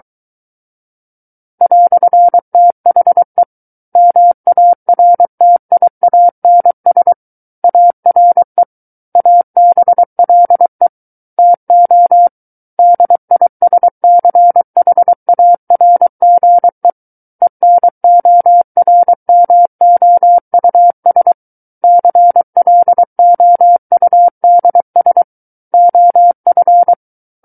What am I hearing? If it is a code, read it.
Morse code, heard as a keyed tone, about 23 wpm, 700 Hz. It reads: "THE MARTIANS ARE ABLE TO DISCHARGE ENORMOUS CLOUDS OF
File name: Book One / 14-Chapter 14 - 23 WPM